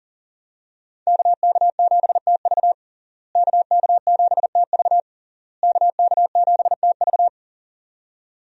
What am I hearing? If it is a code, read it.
Morse code, heard as a keyed tone, about 40 wpm, 700 Hz. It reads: KK7TV KK7TV KK7TV
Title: USA Call Signs - 5 Characters - Repeated 3x - 40wpm